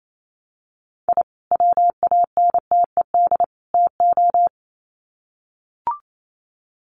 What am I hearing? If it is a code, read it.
Morse code, heard as a keyed tone, about 28 wpm, 700 Hz. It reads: I WANTED TO
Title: Sets of 3 Words 28wpm